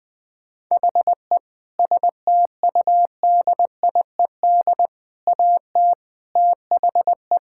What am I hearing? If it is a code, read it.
Morse code, heard as a keyed tone, about 20 wpm, 700 Hz. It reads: HE STUDIED AT THE
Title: Sentences with 4-Word Phrases - 20wpm